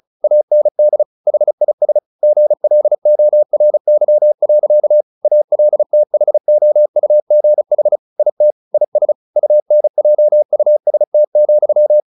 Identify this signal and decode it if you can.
Morse code, heard as a keyed tone, about 34 wpm, 600 Hz. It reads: AND HIS GLORY. ALTHOUGH IT IS UNJUST,
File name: the old man and the sea / the old man and the sea 1 35wpm